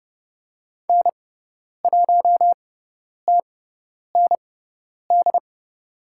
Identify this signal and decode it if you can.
Morse code, heard as a keyed tone, about 30 wpm, spaced as for 10 wpm, 690 Hz. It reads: N1TND
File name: Lesson 17g - Letter u - Call Signs 30wpm